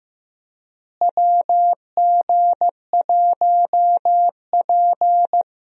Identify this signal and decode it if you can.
Morse code, heard as a keyed tone, about 15 wpm, 700 Hz. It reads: WG1P
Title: USA Call Signs - 4 Characters - Repeated 2x - 15wpm